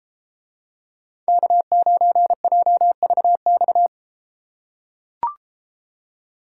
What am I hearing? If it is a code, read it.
Morse code, heard as a keyed tone, about 33 wpm, 700 Hz. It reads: K9JVX